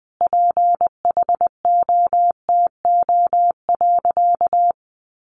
Morse code, heard as a keyed tone, about 20 wpm, 700 Hz. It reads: PHOTO.
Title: Sentences from Top 1200 Words 20wpm